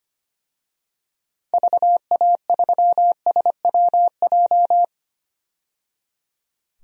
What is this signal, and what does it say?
Morse code, heard as a keyed tone, about 25 wpm, 700 Hz. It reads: VA3SWJ
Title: DXCC-100 Call Signs - 6 Characters - 25wpm